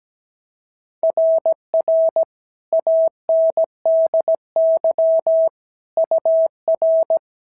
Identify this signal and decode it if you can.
Morse code, heard as a keyed tone, about 17 wpm, 690 Hz. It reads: RR ANDY UR
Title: SOTA QSO Practice 17wpm